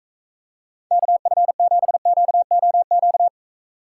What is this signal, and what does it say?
Morse code, heard as a keyed tone, about 43 wpm, 700 Hz. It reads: KF7QOQ